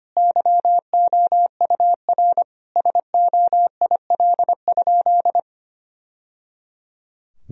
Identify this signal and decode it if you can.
Morse code, heard as a keyed tone, about 25 wpm, 700 Hz. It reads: YOUR SOIL?
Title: Sentences from Top 1000 Words - Review Part 4 - 25wpm